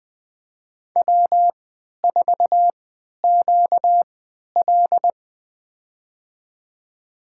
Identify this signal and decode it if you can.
Morse code, heard as a keyed tone, about 20 wpm, spaced as for 11 wpm, 700 Hz. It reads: W4QL
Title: USA Call Signs - 4 Characters - 3x Character Spacing - 20wpm